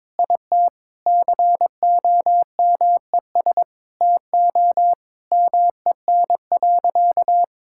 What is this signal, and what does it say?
Morse code, heard as a keyed tone, about 23 wpm, 700 Hz. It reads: IT COMES TO MEN.
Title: Sentences from Top 1000 Words - Review Part 6 - 22wpm